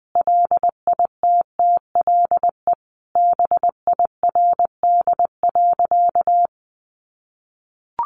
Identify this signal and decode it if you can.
Morse code, heard as a keyed tone, about 20 wpm, 700 Hz. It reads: LITTLE BIRD.
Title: Sentences from Top 1100 Words 20wpm